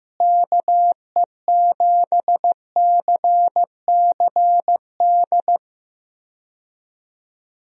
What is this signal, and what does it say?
Morse code, heard as a keyed tone, about 15 wpm, 700 Hz. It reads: KE7CCD
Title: USA Call Signs - 6 Characters - Repeated 2x - 15wpm